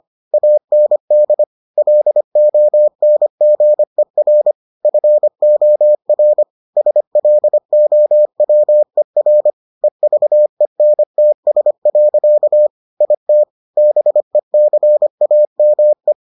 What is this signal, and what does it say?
Morse code, heard as a keyed tone, about 25 wpm, 600 Hz. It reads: AND LONGER FOR SLOWER EVENTS. IT BECAME